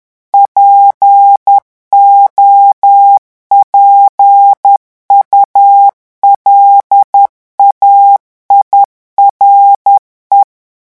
Morse code, heard as a keyed tone, about 10 wpm, 800 Hz. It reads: POPULAIRE
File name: morse mp3 / revision francais 10 mots minute